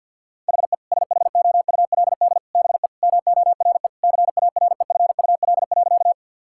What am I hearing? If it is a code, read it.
Morse code, heard as a keyed tone, about 50 wpm, 700 Hz. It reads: HE SHOULD BE MORE CAREFUL.
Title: Sentences from Top 1600 Words 50wpm